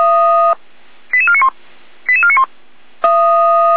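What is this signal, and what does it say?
Five-tone selective call (ZVEI), heard as quick runs of five tones, tones of about 70 ms. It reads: 88491 88491
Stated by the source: ZVEI five-tone selective call received on an SDR.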